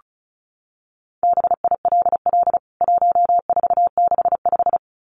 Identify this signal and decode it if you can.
Morse code, heard as a keyed tone, about 34 wpm, 700 Hz. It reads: BILL 1465
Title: CWT Exchange 35wpm